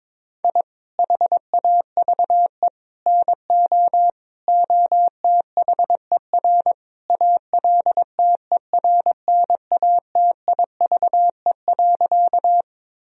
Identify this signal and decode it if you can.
Morse code, heard as a keyed tone, about 21 wpm, 700 Hz. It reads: I HAVE NO OTHER ALTERNATIVE.
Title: Sentences from Top 1000 Words - Review Part 4 - 22wpm